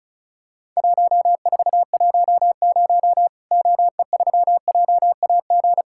Morse code, heard as a keyed tone, about 34 wpm, 700 Hz. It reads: 1410 OE3JAG
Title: Worked All Europe DX Contest - QTC Exchange - 35wpm